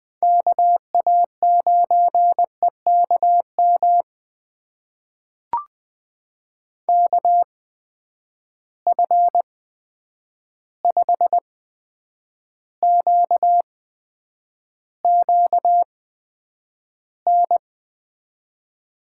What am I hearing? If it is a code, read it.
Morse code, heard as a keyed tone, about 20 wpm, spaced as for 9 wpm, 700 Hz. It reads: KA9EKM KF5QQN
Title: USA Call Signs - 6 Characters - 8x Character Spacing - 20wpm